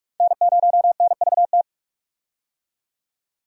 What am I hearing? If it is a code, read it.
Morse code, heard as a keyed tone, about 43 wpm, 700 Hz. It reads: N0NVT
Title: USA Call Signs - 5 Characters - 45wpm